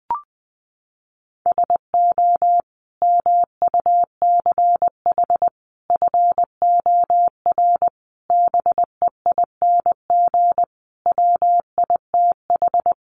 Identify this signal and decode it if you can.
Morse code, heard as a keyed tone, about 20 wpm, 700 Hz. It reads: SO MUCH FOR BEING WITH
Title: Sets of 5 Words Encore 20wpm